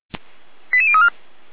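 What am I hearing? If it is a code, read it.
Five-tone selective call (ZVEI), heard as quick runs of five tones, tones of about 70 ms. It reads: 88034
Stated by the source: ZVEI five-tone selective call 89164 received by SDR, from the Feuerwehr Orlamünde audio server.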